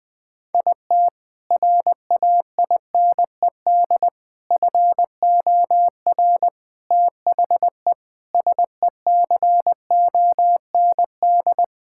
Morse code, heard as a keyed tone, about 21 wpm, 700 Hz. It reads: IT RAINED FOR THE SECOND
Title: Sentences with 7-Word Phrases - 20wpm